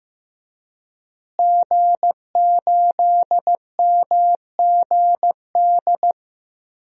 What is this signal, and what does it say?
Morse code, heard as a keyed tone, about 15 wpm, 700 Hz. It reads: G8MGD
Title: DXCC-100 Call Signs - 5 Characters - Repeated 2x - 15wpm